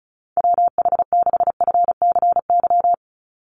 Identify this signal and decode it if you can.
Morse code, heard as a keyed tone, about 34 wpm, 700 Hz. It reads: WH6FCY
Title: USA Call Signs 35wpm